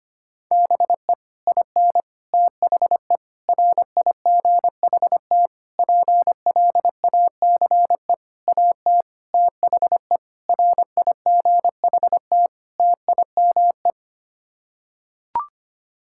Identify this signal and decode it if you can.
Morse code, heard as a keyed tone, about 25 wpm, 700 Hz. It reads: BE IN THE RIGHT PLACE AT THE RIGHT TIME
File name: Sets of 9 Words - 25wpm